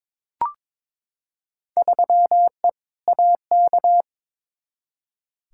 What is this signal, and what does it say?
Morse code, heard as a keyed tone, about 22 wpm, 700 Hz. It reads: 3E AK
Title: Field Day Exchanges Encore - 22wpm